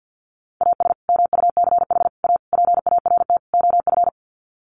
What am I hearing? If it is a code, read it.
Morse code, heard as a keyed tone, about 50 wpm, 700 Hz. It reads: AS MUCH A PART OF
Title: Sets of 5 Words Encore2 50wpm